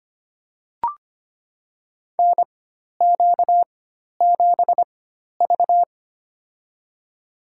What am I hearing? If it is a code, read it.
Morse code, heard as a keyed tone, about 25 wpm, spaced as for 12 wpm, 700 Hz. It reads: NQ7V
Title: USA Call Signs - 4 Characters - 4x Character Spacing - 25wpm